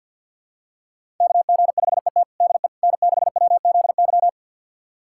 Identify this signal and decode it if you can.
Morse code, heard as a keyed tone, about 51 wpm, 700 Hz. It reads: KG5ET DE N6WZY
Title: USA Callsign DE Callsign - 5 Characters - 50wpm